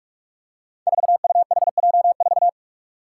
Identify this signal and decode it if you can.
Morse code, heard as a keyed tone, about 44 wpm, 700 Hz. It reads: 4UHJ4
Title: Five Challenging Letter-Number Combinations 45wpm